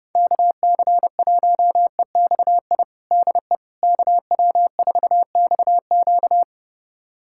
Morse code, heard as a keyed tone, about 30 wpm, 700 Hz. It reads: KC1EXI DE KW4XQ
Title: USA Callsign DE Callsign 30wpm